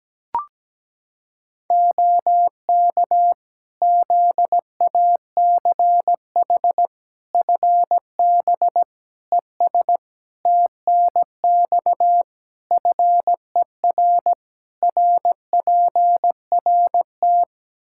Morse code, heard as a keyed tone, about 17 wpm, 700 Hz. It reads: OK ZACH FB ES TNX FER RPRT
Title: Practice QSO - Exchange 2 - Info - 17wpm